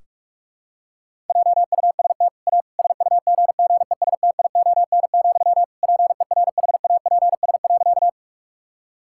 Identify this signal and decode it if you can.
Morse code, heard as a keyed tone, about 46 wpm, 700 Hz. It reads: JUST A SUGGESTION, PERHAPS.
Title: Sentences from Top 1600 Words 45wpm